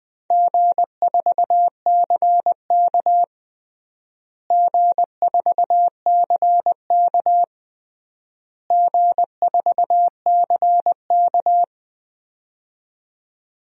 Morse code, heard as a keyed tone, about 20 wpm, 700 Hz. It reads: G4CK G4CK G4CK
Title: DXCC-100 Call Signs - 4 Characters - Repeated 3x - 20wpm